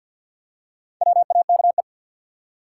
Morse code, heard as a keyed tone, about 50 wpm, 700 Hz. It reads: WAKE
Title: Four Letter Words - 50wpm